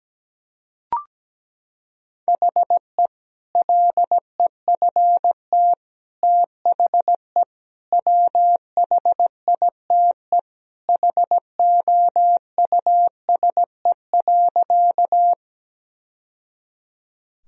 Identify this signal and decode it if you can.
Morse code, heard as a keyed tone, about 17 wpm, 700 Hz. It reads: HE LEFT THE WHITE HOUSE.
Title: Sentences from Top 1000 Words - Review Part 1 - 17wpm